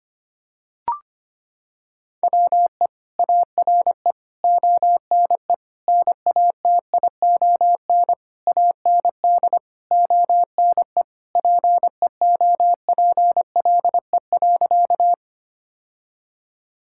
Morse code, heard as a keyed tone, about 25 wpm, 700 Hz. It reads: WE ARE ONE NATION AND ONE PEOPLE.
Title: Sentences from Top 1000 Words - Review Part 6 - 25wpm